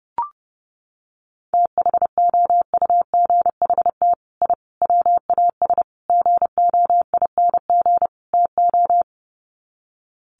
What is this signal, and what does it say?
Morse code, heard as a keyed tone, about 30 wpm, 700 Hz. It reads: THOUGHT I WAS GOING TO
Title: Sets of 5 Words Encore2 30wpm